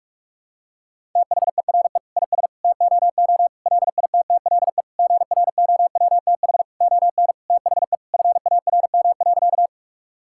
Morse code, heard as a keyed tone, about 46 wpm, 700 Hz. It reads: THERE IS TOO LITTLE GROWTH ON THE FARM.
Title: Sentences from Top 1000 Words - Review Part 5 - 45wpm